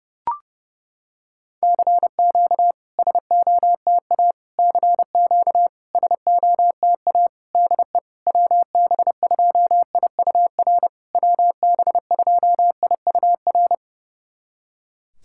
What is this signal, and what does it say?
Morse code, heard as a keyed tone, about 28 wpm, 690 Hz. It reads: CQ SOTA CQ SOTA DE WB2IUR WB2IUR
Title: SOTA QSO Practice 30wpm